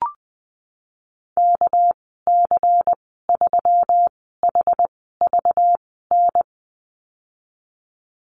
Morse code, heard as a keyed tone, about 20 wpm, spaced as for 15 wpm, 700 Hz. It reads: KC3HVN
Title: 213 Lesson 34g - Number 8 - Call Signs 20wpm